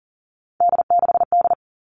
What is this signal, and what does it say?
Morse code, heard as a keyed tone, about 40 wpm, 700 Hz. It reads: D6D
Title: All Cut Numbers - 3-Digit 40wpm